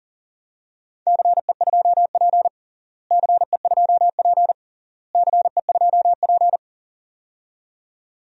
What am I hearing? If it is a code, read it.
Morse code, heard as a keyed tone, about 37 wpm, 700 Hz. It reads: CE2P CE2P CE2P
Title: DXCC-100 Call Signs - 4 Characters - Repeated 3x - 40wpm